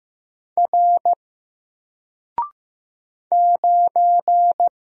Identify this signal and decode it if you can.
Morse code, heard as a keyed tone, about 15 wpm, 700 Hz. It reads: R 9
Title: Single Letter-Number 15wpm